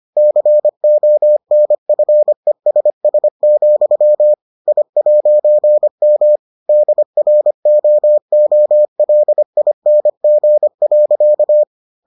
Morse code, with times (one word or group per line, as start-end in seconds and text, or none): 0.2-4.3 CONFESS,
4.7-6.4 I'M
6.7-11.6 DROOLING.